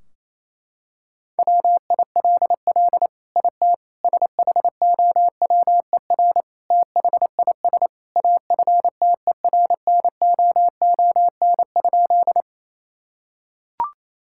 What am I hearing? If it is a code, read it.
Morse code, heard as a keyed tone, about 28 wpm, 700 Hz. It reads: WILL IT SHOWER THIS AFTERNOON?
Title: Sentences from Top 1600 Words 28wpm